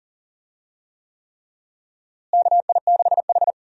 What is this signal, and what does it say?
Morse code, heard as a keyed tone, about 40 wpm, 700 Hz. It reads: KI6H
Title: USA Call Signs - 4 Characters - Repeated 3x - 40wpm